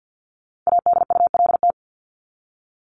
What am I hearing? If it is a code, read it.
Morse code, heard as a keyed tone, about 50 wpm, 700 Hz. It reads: ADULT
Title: Top 1501-1750 Words 50wpm